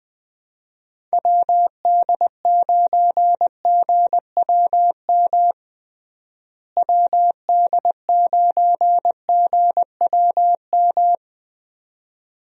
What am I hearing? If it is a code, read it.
Morse code, heard as a keyed tone, about 20 wpm, 700 Hz. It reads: WD9GWM WD9GWM
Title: USA Call Signs - 6 Characters - Repeated 2x - 20wpm